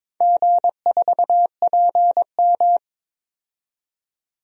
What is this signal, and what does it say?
Morse code, heard as a keyed tone, about 22 wpm, 700 Hz. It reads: G4PM
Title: DXCC-100 Call Signs - 4 Characters - 22wpm